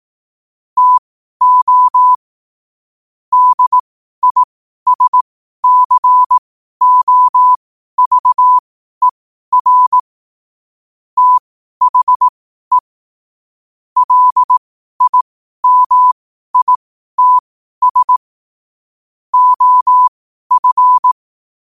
Morse code, heard as a keyed tone, about 18 wpm, spaced as for 12 wpm, 1000 Hz. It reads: TO DISCOVER THE LIMITS OF
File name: qotdfiles / 12wpm